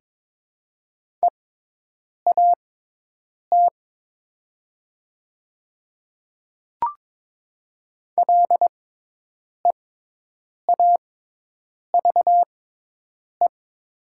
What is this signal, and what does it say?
Morse code, heard as a keyed tone, about 22 wpm, spaced as for 8 wpm, 700 Hz. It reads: EAT LEAVE
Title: 6x Character Spacing - Top 500 Words - 22wpm